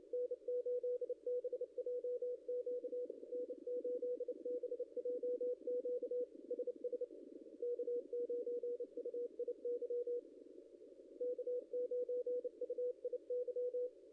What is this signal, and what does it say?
Morse code, heard as a keyed tone, about 28 wpm, 500 Hz. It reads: N8BJQ N8BJQ SS K9UIY K9UIY
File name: contests / ss-cw-k5tr-op-n5rz -0600